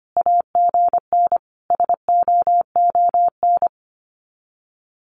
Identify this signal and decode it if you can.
Morse code, heard as a keyed tone, about 25 wpm, 700 Hz. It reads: AGN SOON
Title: Common QSO Phrases 25wpm